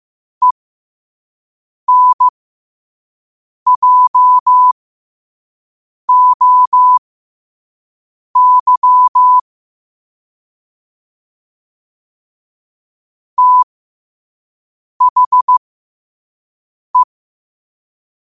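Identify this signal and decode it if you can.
Morse code, heard as a keyed tone, about 15 wpm, spaced as for 5 wpm, 1000 Hz. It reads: ENJOY THE
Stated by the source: Quote-of-the-day Morse code practice at 5 wpm.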